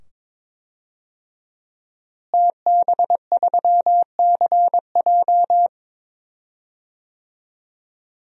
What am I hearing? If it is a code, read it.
Morse code, heard as a keyed tone, about 21 wpm, 700 Hz. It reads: TB3CJ
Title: DXCC-100 Call Signs - 5 Characters - Repeated 2x - 22wpm